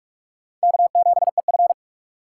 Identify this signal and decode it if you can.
Morse code, heard as a keyed tone, about 44 wpm, 700 Hz. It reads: K7EF